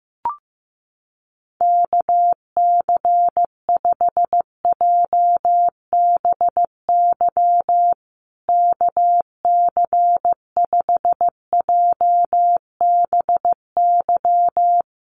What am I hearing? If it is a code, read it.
Morse code, heard as a keyed tone, about 15 wpm, 700 Hz. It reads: KC5JBY KC5JBY
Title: USA Call Signs Repeated 3x 15wpm